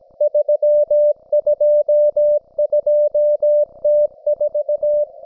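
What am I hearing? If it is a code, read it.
Morse code, heard as a keyed tone, about 17 wpm, 590 Hz. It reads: 322T4
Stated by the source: Russian military Morse station, M32, RDL, on 5268 kHz.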